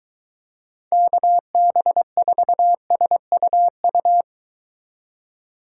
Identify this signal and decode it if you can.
Morse code, heard as a keyed tone, about 22 wpm, 700 Hz. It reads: KB4SUU